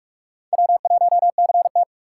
Morse code, heard as a keyed tone, about 46 wpm, 700 Hz. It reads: W1CT